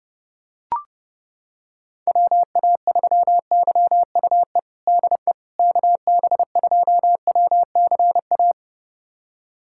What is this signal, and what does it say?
Morse code, heard as a keyed tone, about 30 wpm, 700 Hz. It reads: WA3YUE DE KB2WCA
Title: USA Callsign DE Callsign 30wpm